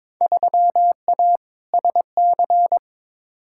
Morse code, heard as a keyed tone, about 22 wpm, 700 Hz. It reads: 3A SC
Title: Field Day Exchanges Encore - 22wpm